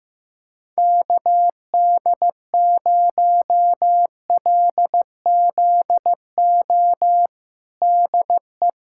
Morse code, heard as a keyed tone, about 16 wpm, 700 Hz. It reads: KD0LZO DE
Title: USA Callsign DE Callsign - 6 Characters - 15wpm